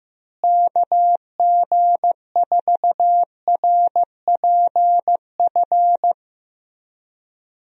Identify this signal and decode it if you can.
Morse code, heard as a keyed tone, about 15 wpm, 700 Hz. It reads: KG4RPF
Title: USA Call Signs - 6 Characters - Repeated 3x - 15wpm